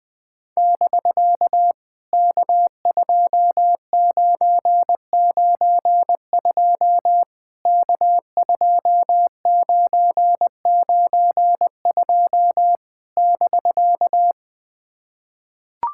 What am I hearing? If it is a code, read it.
Morse code, heard as a keyed tone, about 20 wpm, 700 Hz. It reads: <BK> K2992 K2992 <BK>
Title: POTA QSO Practice - 20wpm